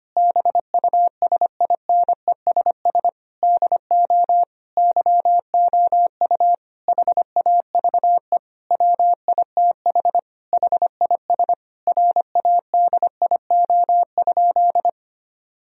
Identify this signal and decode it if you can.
Morse code, heard as a keyed tone, about 25 wpm, 700 Hz. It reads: BUSINESS DO YOU HAVE WITH HIS RADIO?
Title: Sentences from Top 1000 Words - Review Part 3 - 25wpm